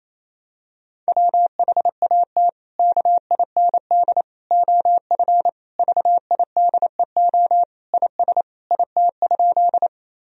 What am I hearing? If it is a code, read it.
Morse code, heard as a keyed tone, about 28 wpm, 700 Hz. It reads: WHAT KIND OF VIDEO IS IT?
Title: Sentences from Top 1000 Words - Review Part 4 - 28wpm